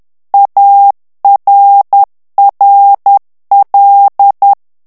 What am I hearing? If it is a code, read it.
Morse code, heard as a keyed tone, about 10 wpm, 800 Hz. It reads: ARRL
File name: ANGLAIS texte clair 3 vitesse 10 mots minute